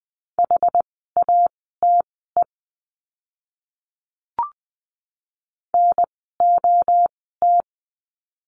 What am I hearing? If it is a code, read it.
Morse code, heard as a keyed tone, about 20 wpm, spaced as for 15 wpm, 700 Hz. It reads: HATE NOT
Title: 061 Lesson 11f - Letter h - Words 20wpm